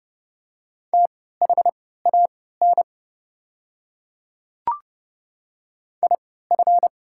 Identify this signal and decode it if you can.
Morse code, heard as a keyed tone, about 30 wpm, spaced as for 17 wpm, 700 Hz. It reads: THAN IF
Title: 3x Character Spacing - Top 100 Words - 30wpm